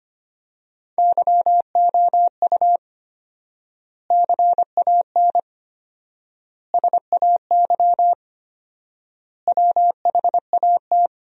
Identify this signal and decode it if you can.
Morse code, heard as a keyed tone, about 25 wpm, 700 Hz. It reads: YOU CAN SAY WHAT
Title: 4x Word-Spacing - Sentences from Top 200 Words - 25wpm